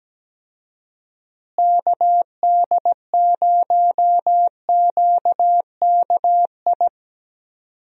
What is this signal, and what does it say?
Morse code, heard as a keyed tone, about 17 wpm, 700 Hz. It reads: KD0QKI
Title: USA Call Signs - 6 Characters - Repeated 2x - 17wpm